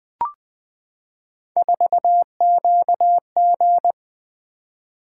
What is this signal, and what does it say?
Morse code, heard as a keyed tone, about 20 wpm, 700 Hz. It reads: 4QG
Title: Three Letter-Number Combinations - 20wpm